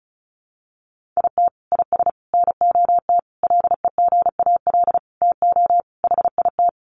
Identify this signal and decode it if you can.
Morse code, heard as a keyed tone, about 34 wpm, 700 Hz. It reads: IT IS NOT LEGAL TO HIT
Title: Sentences from Top 1200 Words 35wpm